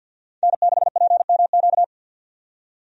Morse code, heard as a keyed tone, about 51 wpm, 700 Hz. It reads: N6PMQ
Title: USA Call Signs - 5 Characters - 50wpm